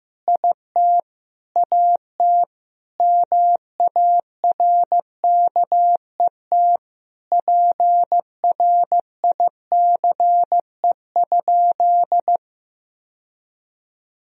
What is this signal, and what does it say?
Morse code, heard as a keyed tone, about 16 wpm, 700 Hz. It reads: IT AT MARKET PRICE?
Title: Sentences from Top 1000 Words - Review Part 3 - 15wpm